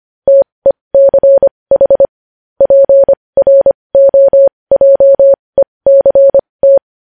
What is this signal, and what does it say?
Morse code, heard as a keyed tone, about 25 wpm, spaced as for 20 wpm, 550 Hz. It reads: TECH PROJECT
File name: podcast / News headlines-1x-25x20